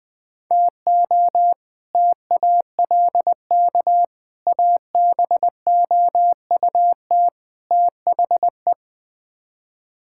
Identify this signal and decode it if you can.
Morse code, heard as a keyed tone, about 21 wpm, 700 Hz. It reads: TO TALK ABOUT THE
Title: Build A Phrase - 4-Word Phrases - 20wpm